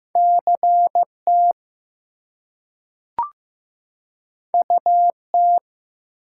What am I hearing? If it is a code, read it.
Morse code, heard as a keyed tone, about 15 wpm, 700 Hz. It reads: CT UT
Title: US States and Canadian Provinces 15wpm